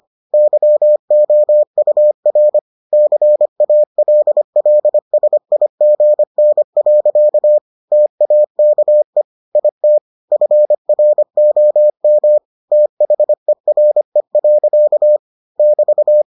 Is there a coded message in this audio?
Yes: Morse code from 0.3 to 16.2 s, heard as a keyed tone, about 25 wpm, 600 Hz. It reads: YOUR CALLSIGN. TAKE IT FROM THERE. <BT>